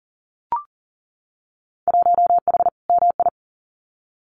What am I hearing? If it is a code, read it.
Morse code, heard as a keyed tone, about 39 wpm, 700 Hz. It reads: 1H MI
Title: Winter Field Day Exchanges 40wpm